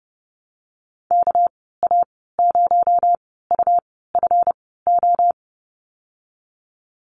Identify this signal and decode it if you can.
Morse code, heard as a keyed tone, about 30 wpm, spaced as for 17 wpm, 700 Hz. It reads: KA0UFO
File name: Lesson 37g - Letter x - Call Signs 30wpm